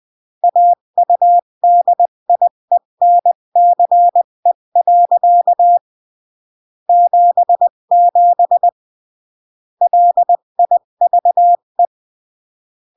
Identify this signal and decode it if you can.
Morse code, heard as a keyed tone, about 20 wpm, 700 Hz. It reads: AUDIENCE. 77 LIVE
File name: CW / jodorowsky / 82 commandments 20-15